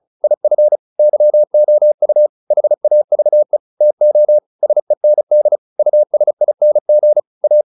Morse code, heard as a keyed tone, about 34 wpm, 600 Hz. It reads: IF YOU HAVE TO SEND USING A